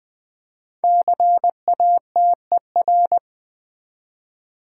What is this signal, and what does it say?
Morse code, heard as a keyed tone, about 20 wpm, 700 Hz. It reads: CATER
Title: Build A Word - 1 Letter At A Time - 3 to 5-Letter Words - 20wpm